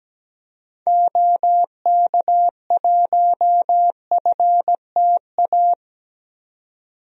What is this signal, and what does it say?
Morse code, heard as a keyed tone, about 17 wpm, 700 Hz. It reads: OK1FTA
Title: DXCC-100 Call Signs - 6 Characters - 17wpm